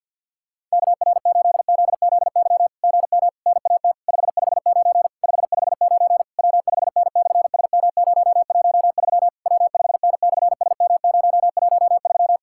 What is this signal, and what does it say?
Morse code, heard as a keyed tone, about 52 wpm, 690 Hz. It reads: KR8ZZY GM NAT 559 559 W5N/SM013 W5N/SM013
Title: SOTA QSO Practice 50wpm